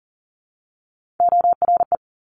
Morse code, heard as a keyed tone, about 40 wpm, 700 Hz. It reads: ORE
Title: Top Three Letter Combinations in Words 40wpm